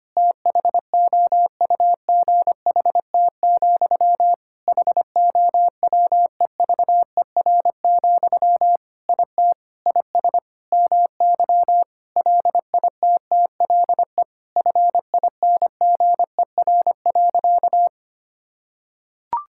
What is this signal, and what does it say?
Morse code, heard as a keyed tone, about 25 wpm, 700 Hz. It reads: THOUGHT, HOWEVER, IT IS MY LITTLE FINGER.
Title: Sentences from Top 1600 Words 25wpm